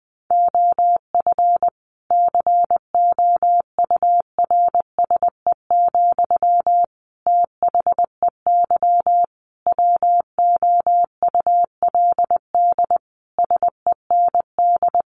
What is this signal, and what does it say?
Morse code, heard as a keyed tone, about 20 wpm, 700 Hz. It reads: OF COURSE, THEY WOULD SEND
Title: Sentences from Top 1500 Words 20wpm